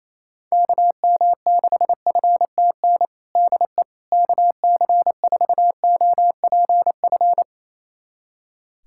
Morse code, heard as a keyed tone, about 28 wpm, 700 Hz. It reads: KM6FTN DE KC4OPF
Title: USA Callsign DE Callsign 28wpm